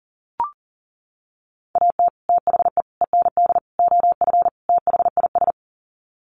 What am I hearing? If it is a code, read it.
Morse code, heard as a keyed tone, about 40 wpm, 700 Hz. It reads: AT THE END OF THIS
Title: Sets of 5 Words 40wpm